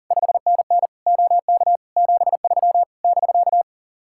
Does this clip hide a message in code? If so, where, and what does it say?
Yes: Morse code from 0.1 to 3.6 s, heard as a keyed tone, about 51 wpm, 700 Hz. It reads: 5NN OK 73 <BK>